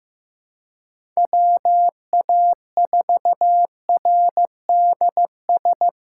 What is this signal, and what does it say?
Morse code, heard as a keyed tone, about 15 wpm, 700 Hz. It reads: WA4RDS
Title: USA Call Signs - 3x Character Spacing - 15wpm